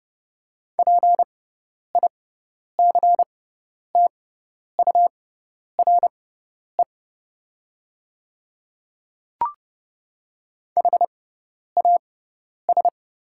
Morse code, heard as a keyed tone, about 32 wpm, spaced as for 10 wpm, 700 Hz. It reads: PICTURE HAS
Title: 6x Character Spacing - Top 200 Words - 30wpm